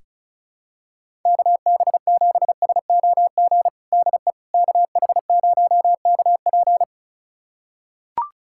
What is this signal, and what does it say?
Morse code, heard as a keyed tone, about 34 wpm, 700 Hz. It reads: KB7SOG DE KH0KP
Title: USA Callsign DE Callsign 35wpm